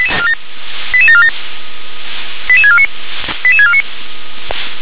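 Five-tone selective call (ZVEI), heard as quick runs of five tones, tones of about 70 ms. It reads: 88747 88649 88649
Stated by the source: ZVEI five-tone selective call 88975 received on SDR.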